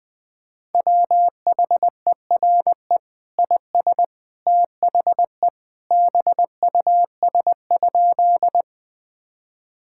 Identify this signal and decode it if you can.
Morse code, heard as a keyed tone, about 21 wpm, 700 Hz. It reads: WHERE IS THE BUS?
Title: Sentences from Top 1000 Words - Review Part 6 - 20wpm